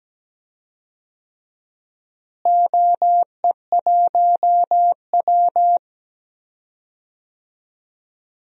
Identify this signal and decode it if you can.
Morse code, heard as a keyed tone, about 17 wpm, 700 Hz. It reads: OE1W
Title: DXCC-100 Call Signs - 4 Characters - Repeated 2x - 17wpm